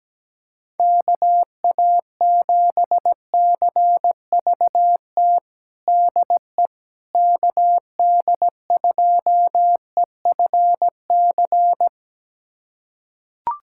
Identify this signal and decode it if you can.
Morse code, heard as a keyed tone, about 17 wpm, 700 Hz. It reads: KA7CVT DE KD2EFC
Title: USA Callsign DE Callsign 17wpm